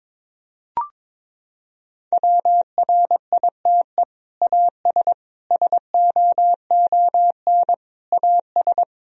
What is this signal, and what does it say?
Morse code, heard as a keyed tone, about 22 wpm, 700 Hz. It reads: WRITE AS SOON AS
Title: Sentences from Top 1000 Words - Review Part 1 - 22wpm